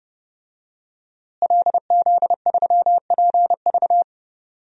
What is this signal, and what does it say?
Morse code, heard as a keyed tone, about 32 wpm, 700 Hz. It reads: LZ3PV
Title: DXCC-100 Call Signs - 5 Characters - Repeated 2x - 30wpm